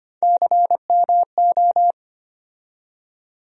Morse code, heard as a keyed tone, about 25 wpm, 700 Hz. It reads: CMO
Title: USA Call Sign Postfixes - 3 Letters 25wpm